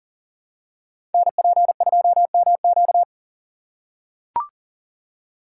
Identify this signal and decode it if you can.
Morse code, heard as a keyed tone, about 40 wpm, 700 Hz. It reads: NP2MQ
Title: Code Groups - 5 Letter:Numbers - 40wpm